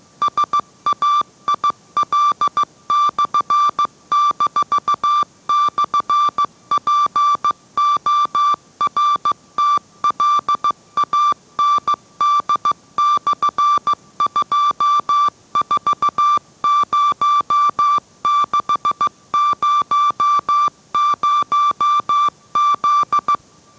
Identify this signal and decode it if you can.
Morse code, heard as a keyed tone, about 15 wpm, 1170 Hz. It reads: SAIL/-/PORTLAND/240600Z